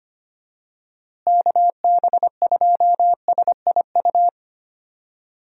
Morse code, heard as a keyed tone, about 25 wpm, 700 Hz. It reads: KB2SIU
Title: USA Call Signs - 6 Characters - Repeated 2x - 25wpm